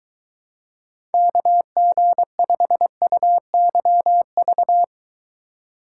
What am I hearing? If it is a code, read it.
Morse code, heard as a keyed tone, about 22 wpm, 700 Hz. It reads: KG5UYV